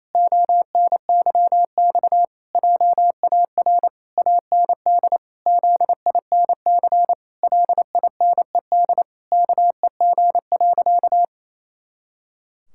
Morse code, heard as a keyed tone, about 28 wpm, 700 Hz. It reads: ONYX JAR AND ZINC LINED KEG.
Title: Pangram Challenge v2 - 28wpm